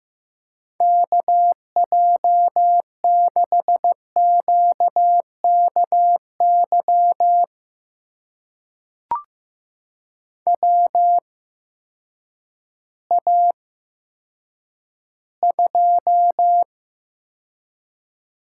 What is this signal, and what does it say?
Morse code, heard as a keyed tone, about 15 wpm, spaced as for 9 wpm, 700 Hz. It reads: KJ6QKY WA2
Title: USA Call Signs - 8x Character Spacing - 15wpm